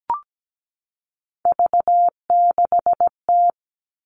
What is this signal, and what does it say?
Morse code, heard as a keyed tone, about 17 wpm, 700 Hz. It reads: V6T
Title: All Cut Numbers - 3-Digit 17wpm